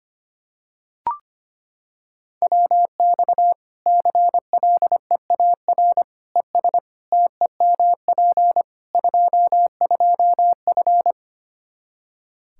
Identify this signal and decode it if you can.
Morse code, heard as a keyed tone, about 25 wpm, 700 Hz. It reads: WX CLEAR ES TEMP 22F
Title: Practice QSO - Exchange 2 - Info - 25wpm